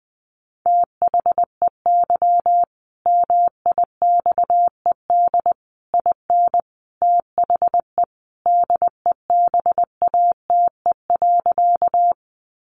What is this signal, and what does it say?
Morse code, heard as a keyed tone, about 20 wpm, 700 Hz. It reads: THEY MIXED IN THE DEBATE.
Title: Sentences from Top 1500 Words 20wpm